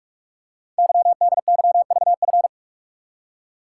Morse code, heard as a keyed tone, about 44 wpm, 700 Hz. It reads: YDYVF
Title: Code Groups - 5 Letters - 45wpm